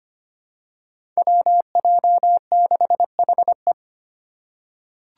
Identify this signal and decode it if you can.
Morse code, heard as a keyed tone, about 25 wpm, 700 Hz. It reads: WJ6HE
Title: Five Challenging Letter-Number Combinations 25wpm